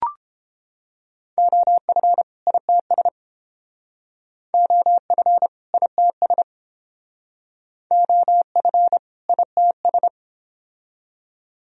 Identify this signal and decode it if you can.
Morse code, heard as a keyed tone, about 29 wpm, 700 Hz. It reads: OF ITS OF ITS OF ITS